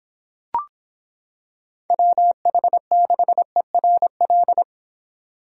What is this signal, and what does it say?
Morse code, heard as a keyed tone, about 26 wpm, 700 Hz. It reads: WH6ERL